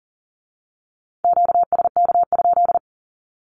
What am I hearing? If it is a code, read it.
Morse code, heard as a keyed tone, about 41 wpm, 700 Hz. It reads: QSK?
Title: Q-Codes with Spoken Meaning - 40wpm